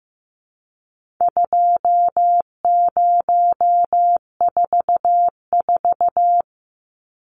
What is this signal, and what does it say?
Morse code, heard as a keyed tone, about 15 wpm, 700 Hz. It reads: 2044
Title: Numbers - 4-Digit 15wpm